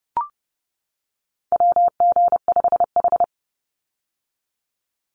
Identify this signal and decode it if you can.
Morse code, heard as a keyed tone, about 30 wpm, 700 Hz. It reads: WG5H
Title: Call Signs - 30wpm